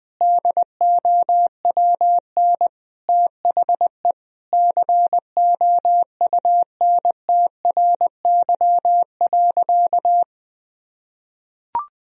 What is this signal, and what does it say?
Morse code, heard as a keyed tone, about 20 wpm, 700 Hz. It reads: DOWN THE COUNTRY.
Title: Sentences from Top 1000 Words - Review Part 2 - 20wpm